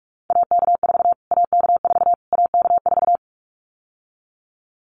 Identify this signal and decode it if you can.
Morse code, heard as a keyed tone, about 44 wpm, 700 Hz. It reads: AK4 AK4 AK4
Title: Callsign Prefixes for Top 25 DXCC - Repeated 3x - 45wpm